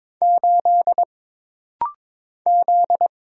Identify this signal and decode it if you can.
Morse code, heard as a keyed tone, about 22 wpm, 700 Hz. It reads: 8 Z
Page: This rapid-fire timing is geared to push your ICR (Instant Character Recognition) skills.